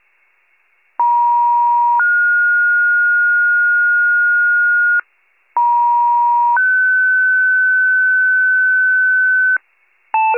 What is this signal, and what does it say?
Two-tone page, heard as a one-second tone, then a three-second tone, about 950 then 1430 Hz, about 950 then 1500 Hz, with noise. It is heard from 1.0 to 9.6 s.